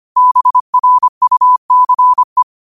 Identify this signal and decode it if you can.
Morse code, heard as a keyed tone, about 25 wpm, 1000 Hz. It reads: BRUCE
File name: qotdfiles / 25wpm